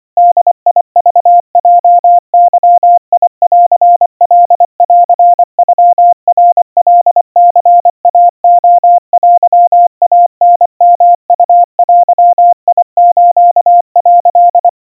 Morse code (Japanese Wabun code), heard as a keyed tone, about 25 wpm, 700 Hz. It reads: ボクヲゲンカンノナカニイレテイタヨウデス」
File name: 90 cpm 700 Hz Crispy